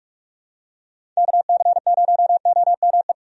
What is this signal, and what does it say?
Morse code, heard as a keyed tone, about 46 wpm, 700 Hz. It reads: KC0OME
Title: USA Call Signs - 6 Characters - 45wpm